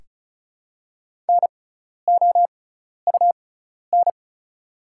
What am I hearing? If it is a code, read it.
Morse code, heard as a keyed tone, about 36 wpm, spaced as for 12 wpm, 700 Hz. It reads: NOUN
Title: 6x Character Spacing - Top 500 Words - 35wpm